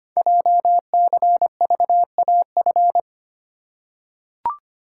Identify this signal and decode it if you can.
Morse code, heard as a keyed tone, about 25 wpm, 700 Hz. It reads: JCVAF
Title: Code Groups - 5 Letters - 25wpm